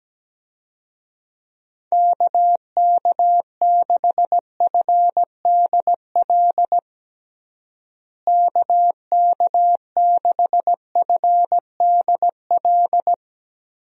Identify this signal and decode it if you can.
Morse code, heard as a keyed tone, about 17 wpm, 700 Hz. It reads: KK6FDL KK6FDL
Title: USA Call Signs - 6 Characters - Repeated 3x - 17wpm